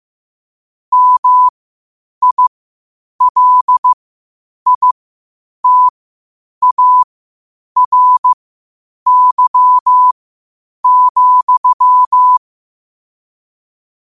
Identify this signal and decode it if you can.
Morse code, heard as a keyed tone, about 15 wpm, spaced as for 8 wpm, 1000 Hz. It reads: MILITARY,
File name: qotdfiles / 8wpm